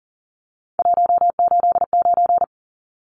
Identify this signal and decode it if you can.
Morse code, heard as a keyed tone, about 40 wpm, 700 Hz. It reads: 189
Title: Numbers - 3-Digit 40wpm